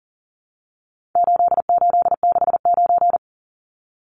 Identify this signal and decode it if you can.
Morse code, heard as a keyed tone, about 40 wpm, 700 Hz. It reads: 8869
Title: Numbers - 4-Digit 40wpm